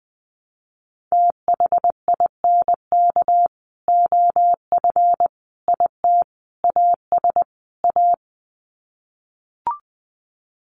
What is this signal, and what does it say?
Morse code, heard as a keyed tone, about 20 wpm, 700 Hz. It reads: THINK OF IT AS A
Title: Sets of 5 Words Encore2 20wpm